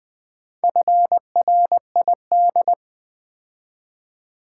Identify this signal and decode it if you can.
Morse code, heard as a keyed tone, about 20 wpm, 700 Hz. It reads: FRID
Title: Build A Word - 2 Letters At A Time - 6-Letter Words - 20wpm